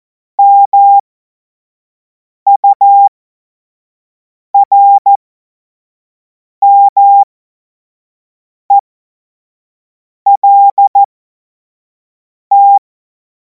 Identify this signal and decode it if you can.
Morse code, heard as a keyed tone, about 14 wpm, spaced as for 5 wpm, 800 Hz. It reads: MURMELT
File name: Morsekurs / Witze / Witz 17. Februar 2026 25 F70